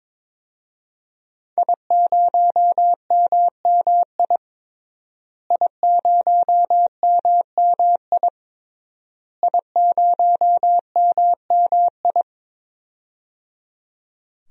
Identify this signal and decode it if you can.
Morse code, heard as a keyed tone, about 22 wpm, 700 Hz. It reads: I0MMI I0MMI I0MMI
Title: DXCC-100 Call Signs - 5 Characters - Repeated 3x - 22wpm